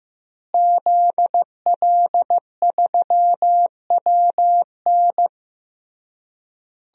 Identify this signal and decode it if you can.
Morse code, heard as a keyed tone, about 15 wpm, 700 Hz. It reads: ZL3WN
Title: DXCC-100 Call Signs - 5 Characters - Repeated 3x - 15wpm